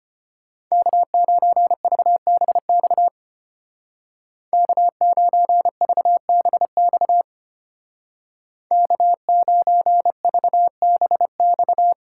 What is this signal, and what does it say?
Morse code, heard as a keyed tone, about 33 wpm, 700 Hz. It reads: K9VBX K9VBX K9VBX